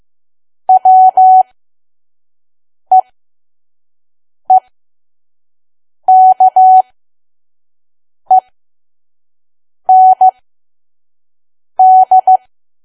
Morse code, heard as a keyed tone, about 15 wpm, spaced as for 5 wpm, 750 Hz. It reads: WEEKEND
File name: Archive / 5 WPM files / PM